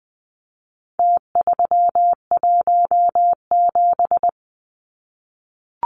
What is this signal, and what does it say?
Morse code, heard as a keyed tone, about 20 wpm, 700 Hz. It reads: T317
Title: Numbers - Contest Style - 4-Digit 20wpm